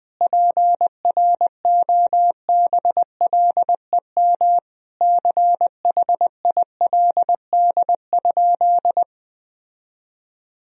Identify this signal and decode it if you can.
Morse code, heard as a keyed tone, about 20 wpm, 700 Hz. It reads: PROBLEM CHILD?
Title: Sentences from Top 1000 Words - Review Part 2 - 20wpm